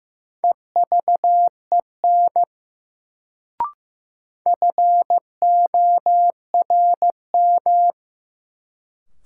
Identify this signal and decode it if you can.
Morse code, heard as a keyed tone, about 15 wpm, 700 Hz. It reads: EVEN FORM
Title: Sets of 4 Words - New Word Drill - 15wpm